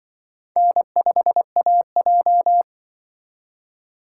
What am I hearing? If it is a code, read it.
Morse code, heard as a keyed tone, about 25 wpm, 700 Hz. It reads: N5AJ
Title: Speed Racing - USA 1x2 Call Signs 20wpm